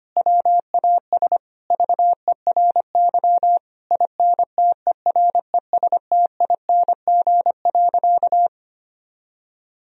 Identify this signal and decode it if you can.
Morse code, heard as a keyed tone, about 25 wpm, 700 Hz. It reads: WAS VERY INTERESTING.
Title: Sentences from Top 1000 Words - Review Part 4 - 25wpm